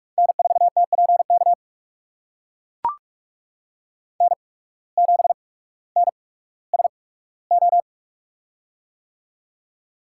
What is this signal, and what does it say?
Morse code, heard as a keyed tone, about 43 wpm, spaced as for 19 wpm, 700 Hz. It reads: N4TPK N7NSO
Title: USA Call Signs - 5 Characters - 8x Character Spacing - 45wpm